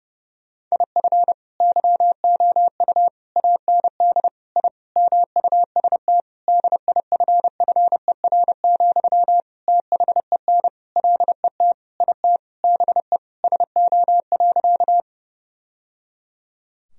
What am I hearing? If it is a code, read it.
Morse code, heard as a keyed tone, about 30 wpm, 700 Hz. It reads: IF YOU AND I MUST DIFFER, THEN LET IT BE SO.
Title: Sentences from Top 1000 Words - Review Part 1 - 30wpm